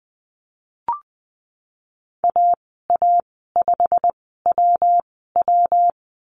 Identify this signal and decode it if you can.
Morse code, heard as a keyed tone, about 20 wpm, spaced as for 15 wpm, 700 Hz. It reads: AA5WW
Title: 161 Lesson 26g - Letter g - Call Signs 20wpm